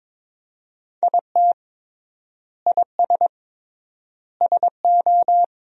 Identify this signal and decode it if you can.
Morse code, heard as a keyed tone, about 22 wpm, 700 Hz. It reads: IT IS SO
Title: 3x Word-Spacing - Sentences from Top 300 Words - 22wpm